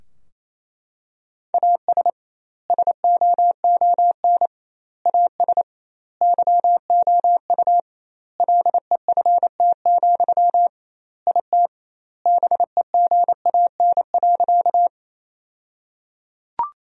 Morse code, heard as a keyed tone, about 28 wpm, 700 Hz. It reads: AS SOON AS YOU LEFT, IT BEGAN.
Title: 2x Word-Spacing - Sentences from Top 400 Words - 28wpm